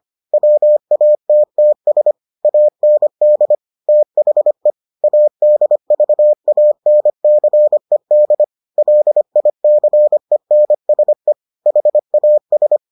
Morse code, heard as a keyed tone, about 25 wpm, 600 Hz. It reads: WATTS AND THE ADVANCED LICENSE HAS